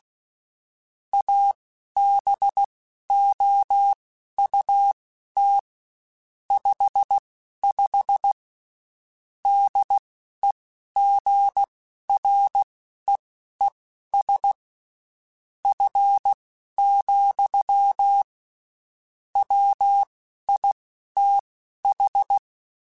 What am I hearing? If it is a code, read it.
Morse code, heard as a keyed tone, about 16 wpm, spaced as for 11 wpm, 800 Hz. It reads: ABOUT 55 DEGREES F, WITH
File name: Spaced Characters / Spaced Characters-mp3s / Weather4 24 sp @15